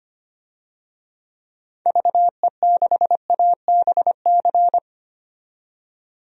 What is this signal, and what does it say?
Morse code, heard as a keyed tone, about 25 wpm, 700 Hz. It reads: VE6ABC
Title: DXCC-100 Call Signs - 6 Characters - 25wpm